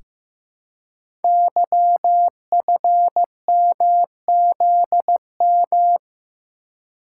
Morse code, heard as a keyed tone, about 15 wpm, 700 Hz. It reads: YFMZM
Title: Code Groups - 5 Letters - 15wpm